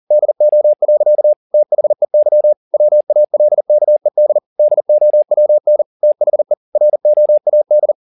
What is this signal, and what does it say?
Morse code, heard as a keyed tone, about 40 wpm, 600 Hz. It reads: DO. THEY WALKED DOWN THE ROAD